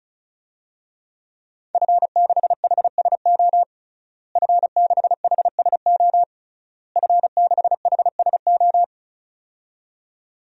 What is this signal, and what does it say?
Morse code, heard as a keyed tone, about 34 wpm, 700 Hz. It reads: F6HSO F6HSO F6HSO
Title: DXCC-100 Call Signs - 5 Characters - Repeated 3x - 35wpm